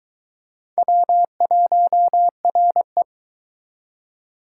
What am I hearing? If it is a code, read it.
Morse code, heard as a keyed tone, about 24 wpm, 700 Hz. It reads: W1RE